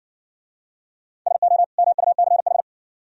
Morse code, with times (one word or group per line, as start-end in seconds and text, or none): 1.3-1.6 SO
1.8-2.6 MUCH